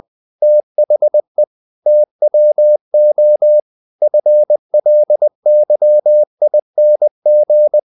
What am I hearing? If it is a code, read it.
Morse code, heard as a keyed tone, about 20 wpm, 600 Hz. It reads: THE TWO FLYING
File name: the old man and the sea part 2 20wpm